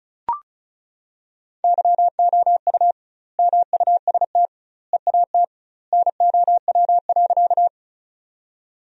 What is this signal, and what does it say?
Morse code, heard as a keyed tone, about 34 wpm, 700 Hz. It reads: YOU MUST EAT NOW.
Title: 2x Word-Spacing - Sentences from Top 300 Words - 35wpm